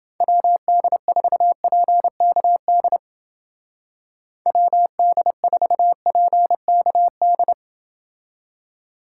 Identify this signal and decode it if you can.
Morse code, heard as a keyed tone, about 28 wpm, 700 Hz. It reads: WD4PKD WD4PKD